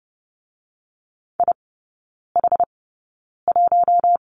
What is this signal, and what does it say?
Morse code, heard as a keyed tone, about 30 wpm, spaced as for 9 wpm, 700 Hz. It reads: IH1
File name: 081 Lesson 14e - Number 2 - Three Characters 30wpm